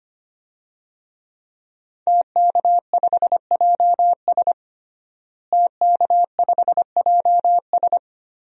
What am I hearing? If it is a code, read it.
Morse code, heard as a keyed tone, about 25 wpm, 700 Hz. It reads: TK5JS TK5JS
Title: DXCC-100 Call Signs - 5 Characters - Repeated 3x - 25wpm